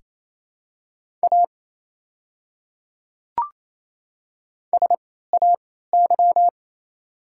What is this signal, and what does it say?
Morse code, heard as a keyed tone, about 28 wpm, spaced as for 16 wpm, 690 Hz. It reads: A SAY
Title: 3x Character Spacing - Top 300 Words - 28wpm